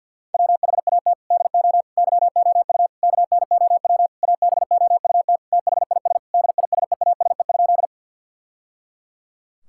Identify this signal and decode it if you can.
Morse code, heard as a keyed tone, about 50 wpm, 700 Hz. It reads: WHAT DO YOU KNOW ABOUT THIS DISEASE?
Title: Sentences from Top 1000 Words - Review Part 6 - 50wpm